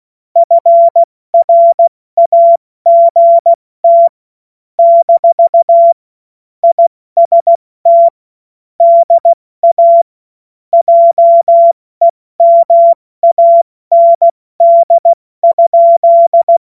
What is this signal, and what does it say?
Morse code, heard as a keyed tone, about 16 wpm, 680 Hz. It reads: FRAGT - IST DA JEMAND?
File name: Morsekurs / Witze / Witz 11. Januar 2022, 70 F80